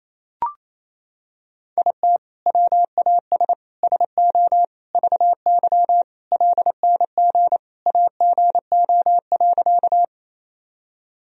Morse code, heard as a keyed tone, about 28 wpm, 700 Hz. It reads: IT WAS SO VY LNG AGO.
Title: Sentences from Top 500 Words and Abbreviations 28wpm